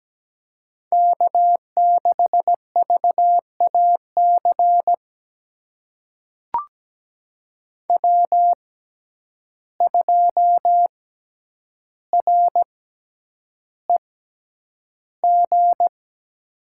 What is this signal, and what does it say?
Morse code, heard as a keyed tone, about 17 wpm, spaced as for 9 wpm, 700 Hz. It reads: K6VAC W2REG
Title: USA Call Signs - 5 Characters - 6x Character Spacing - 17wpm